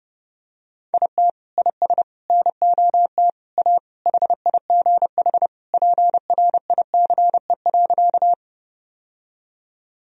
Morse code, heard as a keyed tone, about 30 wpm, 700 Hz. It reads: IT IS NOT A HIGH PRICE.
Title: Sentences from Top 1000 Words - Review Part 4 - 30wpm